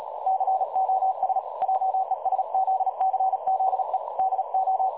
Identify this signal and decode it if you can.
Morse code, heard as a keyed tone, about 38 wpm, 760 Hz, with noise. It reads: COS JESZCZE DO
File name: dlazaawansowanych / nagraniazpasma / websdr recording 3565.0kHz